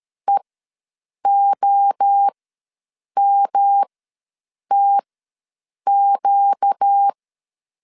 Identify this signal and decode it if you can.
Morse code, heard as a keyed tone, about 13 wpm, spaced as for 7 wpm, 800 Hz. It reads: EOMTQ
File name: 5 LB ws 12wpm